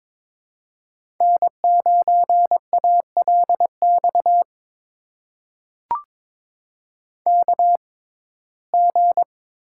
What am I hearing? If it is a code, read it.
Morse code, heard as a keyed tone, about 22 wpm, spaced as for 16 wpm, 700 Hz. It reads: N9ALX KG
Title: USA Call Signs - 5 Characters - 6x Character Spacing - 22wpm